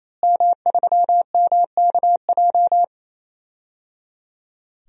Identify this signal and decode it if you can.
Morse code, heard as a keyed tone, about 28 wpm, 700 Hz. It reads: M3MKJ
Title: DXCC-100 Call Signs - 5 Characters - Repeated 3x - 28wpm